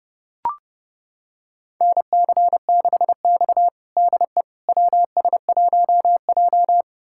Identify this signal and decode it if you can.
Morse code, heard as a keyed tone, about 32 wpm, 700 Hz. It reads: NC6X DE WS1J
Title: USA Callsign DE Callsign - 4 Characters - 30wpm